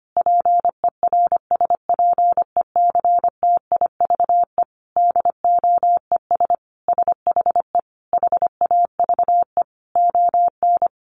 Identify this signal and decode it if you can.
Morse code, heard as a keyed tone, about 25 wpm, 700 Hz. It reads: PERSPECTIVE DOES SHE HAVE ON
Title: Sentences from Top 1200 Words 25wpm